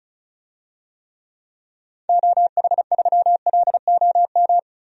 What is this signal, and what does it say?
Morse code, heard as a keyed tone, about 34 wpm, 700 Hz. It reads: OH3LOM
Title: DXCC-100 Call Signs - 6 Characters - 35wpm